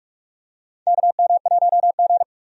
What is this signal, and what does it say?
Morse code, heard as a keyed tone, about 46 wpm, 700 Hz. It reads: KM1G
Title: USA Call Signs - 4 Characters - 45wpm